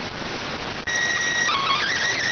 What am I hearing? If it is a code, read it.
SSTV (slow-scan television), PD 290.